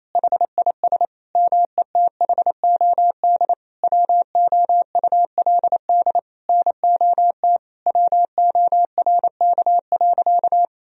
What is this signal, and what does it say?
Morse code, heard as a keyed tone, about 28 wpm, 700 Hz. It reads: HIS METHOD WOULD NOT WORK.
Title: Sentences from Top 1000 Words - Review Part 3 - 28wpm